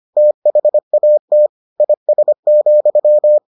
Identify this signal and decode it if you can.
Morse code, heard as a keyed tone, about 25 wpm, 600 Hz. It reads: THAT IS,